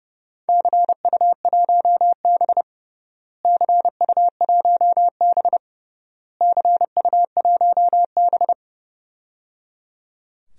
Morse code, heard as a keyed tone, about 30 wpm, 700 Hz. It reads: CU1B CU1B CU1B
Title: DXCC-100 Call Signs - 4 Characters - Repeated 3x - 30wpm